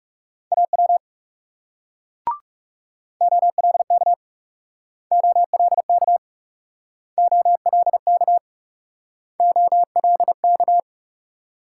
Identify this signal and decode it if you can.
Morse code, heard as a keyed tone, about 40 wpm, 700 Hz. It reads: AW OLK OLK OLK OLK